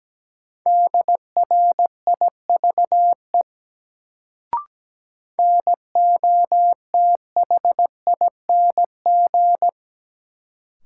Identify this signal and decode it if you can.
Morse code, heard as a keyed tone, about 17 wpm, 700 Hz. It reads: DRIVE NOTHING
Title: Sentences from Top 500 Words - New Word Drill - 17wpm